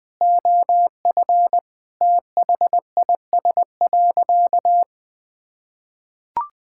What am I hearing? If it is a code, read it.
Morse code, heard as a keyed tone, about 20 wpm, 700 Hz. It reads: OF THIS.
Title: Sentences with 5-Word Phrases - 20wpm